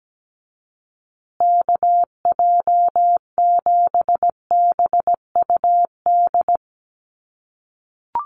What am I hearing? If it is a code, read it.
Morse code, heard as a keyed tone, about 17 wpm, 700 Hz. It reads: KJ7BUD
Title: USA Call Signs Repeated 2x 17wpm